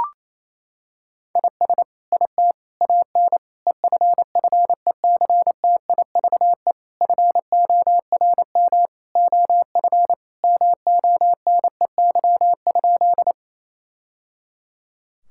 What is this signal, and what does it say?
Morse code, heard as a keyed tone, about 27 wpm, 700 Hz. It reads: IS IT AN EFFECTIVE FORM OF MONEY?
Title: Sentences from Top 1000 Words - Review Part 6 - 28wpm